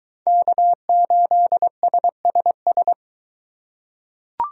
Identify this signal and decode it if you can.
Morse code, heard as a keyed tone, about 22 wpm, 700 Hz. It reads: K8SSS